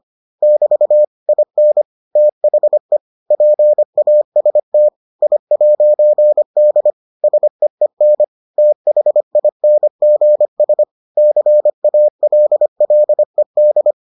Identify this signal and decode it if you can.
Morse code, heard as a keyed tone, about 25 wpm, 600 Hz. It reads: <BT> IN THE PAST I'D SEEN THINGS CALLED